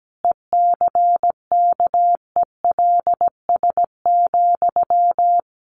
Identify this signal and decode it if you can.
Morse code, heard as a keyed tone, about 17 wpm, 700 Hz. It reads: ECKELS,
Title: A Sound Of Thunder 17wpm